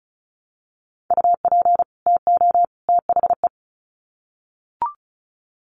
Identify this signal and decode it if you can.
Morse code, heard as a keyed tone, about 34 wpm, 700 Hz. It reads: UP TO THE
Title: Sets of 3 Words 35wpm